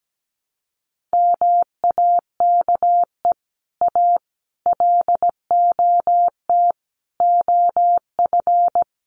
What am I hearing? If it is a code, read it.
Morse code, heard as a keyed tone, about 17 wpm, 700 Hz. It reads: MAKE A LOT OF
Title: Sets of 5 Words Encore2 17wpm